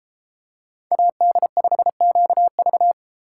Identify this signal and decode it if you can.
Morse code, heard as a keyed tone, about 31 wpm, 700 Hz. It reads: AD5QV